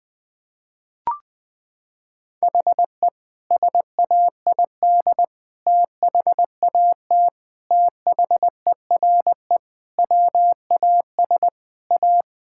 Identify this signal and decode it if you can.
Morse code, heard as a keyed tone, about 21 wpm, 700 Hz. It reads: HE SAID THAT THERE WAS A
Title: Sentences with 3-Word Phrases - 20wpm